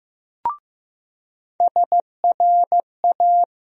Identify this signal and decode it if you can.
Morse code, heard as a keyed tone, about 16 wpm, 700 Hz. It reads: SRA
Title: USA Call Sign Postfixes - 3 Letters 15wpm